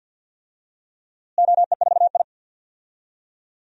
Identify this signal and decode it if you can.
Morse code, heard as a keyed tone, about 47 wpm, 700 Hz. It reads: OE4I
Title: DXCC-100 Call Signs - 4 Characters - Repeated 3x - 50wpm